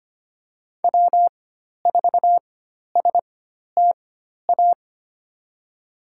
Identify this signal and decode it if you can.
Morse code, heard as a keyed tone, about 25 wpm, spaced as for 12 wpm, 700 Hz. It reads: W4STA
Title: USA Call Signs - 5 Characters - 4x Character Spacing - 25wpm